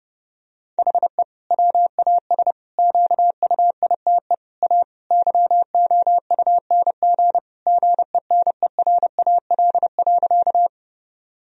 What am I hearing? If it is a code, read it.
Morse code, heard as a keyed tone, about 30 wpm, 700 Hz. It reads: HE WAS QUITE A YOUNG GENERAL.
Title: Sentences from Top 1000 Words - Review Part 3 - 30wpm